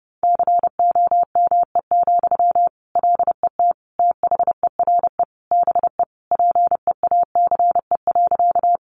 Morse code, heard as a keyed tone, about 28 wpm, 700 Hz. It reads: COME, LET THERE BE PEACE.
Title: Sentences from Top 1500 Words 30wpm